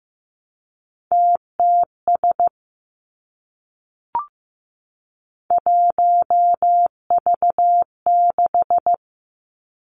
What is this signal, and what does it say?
Morse code, heard as a keyed tone, about 15 wpm, 700 Hz. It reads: TTS 1V6
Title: Three Challenging Letter-Number Combinations 15wpm